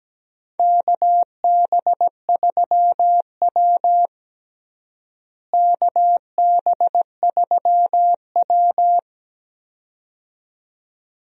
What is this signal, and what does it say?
Morse code, heard as a keyed tone, about 17 wpm, 700 Hz. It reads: KB3W KB3W
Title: USA Call Signs - 4 Characters - Repeated 2x - 17wpm